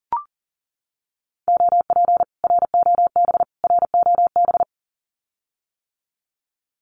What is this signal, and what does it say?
Morse code, heard as a keyed tone, about 38 wpm, 700 Hz. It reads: OP ROB ROB
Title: Common QSO Phrases Encore 40wpm